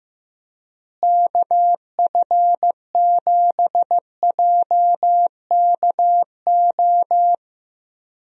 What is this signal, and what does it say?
Morse code, heard as a keyed tone, about 15 wpm, 700 Hz. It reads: KF7JKO
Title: USA Call Signs - 6 Characters - Repeated 2x - 15wpm